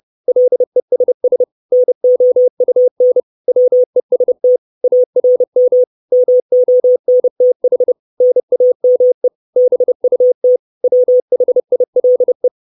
Morse code, heard as a keyed tone, about 30 wpm, 500 Hz. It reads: LESS NOUN WEST ARM MONTH NAME BUT WHILE
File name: 632 Words 30WPM